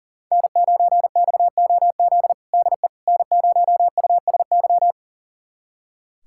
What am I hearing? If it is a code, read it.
Morse code, heard as a keyed tone, about 37 wpm, 700 Hz. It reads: N9XOZ DE N0USY
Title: USA Callsign DE Callsign - 5 Characters - 40wpm